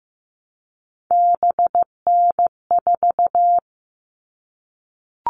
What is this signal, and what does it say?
Morse code, heard as a keyed tone, about 15 wpm, 700 Hz. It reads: BN4
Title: All Cut Numbers - 3-Digit 15wpm